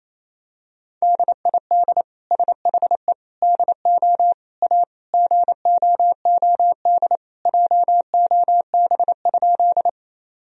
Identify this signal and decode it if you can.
Morse code, heard as a keyed tone, about 28 wpm, 700 Hz. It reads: DID SHE DO A GOOD JOB?
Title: Sentences from Top 1000 Words - Review Part 3 - 28wpm